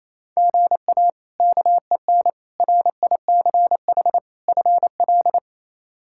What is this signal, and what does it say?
Morse code, heard as a keyed tone, about 27 wpm, 700 Hz. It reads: GA KEN RICH FL
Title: K1USN SST QSO Practice - 28wpm